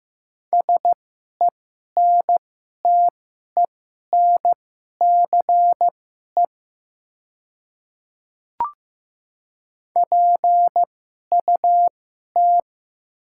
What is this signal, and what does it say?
Morse code, heard as a keyed tone, about 15 wpm, spaced as for 11 wpm, 700 Hz. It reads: SENTENCE PUT
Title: 2x Character Spacing - Top 300 Words - 15wpm